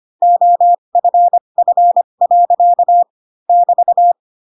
Morse code, heard as a keyed tone, about 25 wpm, spaced as for 22 wpm, 700 Hz. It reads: OFF. <BT>
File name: Pinocchio / Pinocchio ch6 18 25 ik1hge